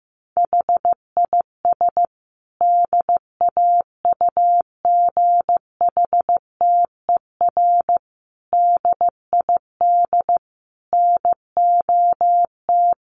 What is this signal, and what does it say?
Morse code, heard as a keyed tone, about 15 wpm, 700 Hz. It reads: HIS DAUGHTER DID NOT
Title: Sentences from Top 1500 Words 15wpm